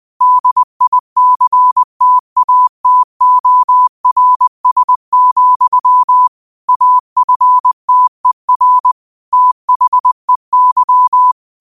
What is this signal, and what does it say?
Morse code, heard as a keyed tone, about 20 wpm, 1000 Hz. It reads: DICTATORS, AFTER THEY
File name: qotdfiles / 20wpm